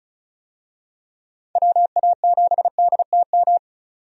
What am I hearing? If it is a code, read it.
Morse code, heard as a keyed tone, about 34 wpm, 700 Hz. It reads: WA7DTM
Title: USA Call Signs - 6 Characters - 35wpm